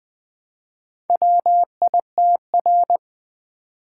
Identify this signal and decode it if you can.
Morse code, heard as a keyed tone, about 20 wpm, 700 Hz. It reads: WITR
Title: Four Letter-Number Combinations - 20wpm